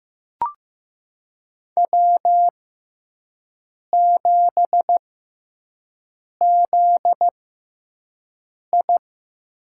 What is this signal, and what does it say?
Morse code, heard as a keyed tone, about 15 wpm, spaced as for 5 wpm, 700 Hz. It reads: W7ZI
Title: USA Call Signs - 4 Characters - 6x Character Spacing - 15wpm